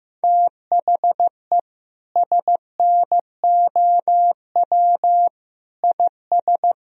Morse code, heard as a keyed tone, about 15 wpm, 700 Hz. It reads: THE SNOW IS
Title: Sentences from Top 1000 Words - Review Part 3 - 15wpm